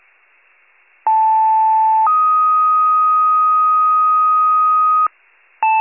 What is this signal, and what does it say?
Two-tone page, heard as a one-second tone, then a three-second tone, about 880 then 1250 Hz. It is heard from 1.1 to 5.1 s.
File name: wcalert